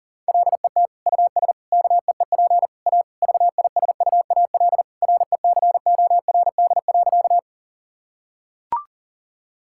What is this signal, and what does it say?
Morse code, heard as a keyed tone, about 37 wpm, 700 Hz. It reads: LET US KEEP A VISUAL RECORD.
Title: Sentences from Top 1000 Words - Review Part 6 - 40wpm